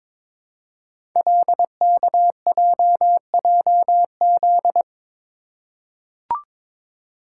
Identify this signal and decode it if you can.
Morse code, heard as a keyed tone, about 22 wpm, 700 Hz. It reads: LKJJZ
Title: Code Groups - 5 Letters - 22wpm